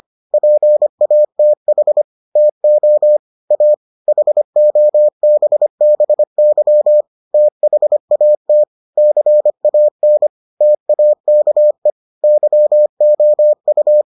Morse code, heard as a keyed tone, about 25 wpm, 600 Hz. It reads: PATH TO A HOBBY THAT CAN TAKE YOU